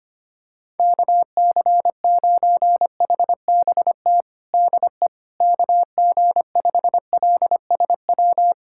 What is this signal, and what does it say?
Morse code, heard as a keyed tone, about 25 wpm, 700 Hz. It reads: KC9HBT DE KG5LSW
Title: USA Callsign DE Callsign 25wpm